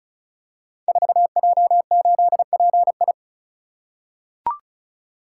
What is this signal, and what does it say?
Morse code, heard as a keyed tone, about 36 wpm, 690 Hz. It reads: 4J8PI
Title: Code Groups - 5 Letter:Numbers - 35wpm